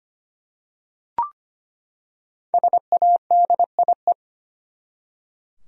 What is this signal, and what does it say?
Morse code, heard as a keyed tone, about 25 wpm, 700 Hz. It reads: SADIE
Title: USA Common Names - Max 5 Letters - 25wpm